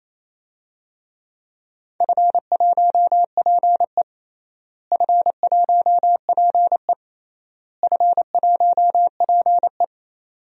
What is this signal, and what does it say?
Morse code, heard as a keyed tone, about 28 wpm, 700 Hz. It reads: F1PE F1PE F1PE
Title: DXCC-100 Call Signs - 4 Characters - Repeated 3x - 28wpm